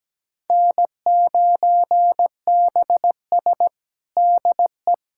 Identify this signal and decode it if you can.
Morse code, heard as a keyed tone, about 17 wpm, 700 Hz. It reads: N9BS DE
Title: USA Callsign DE Callsign - 4 Characters - 17wpm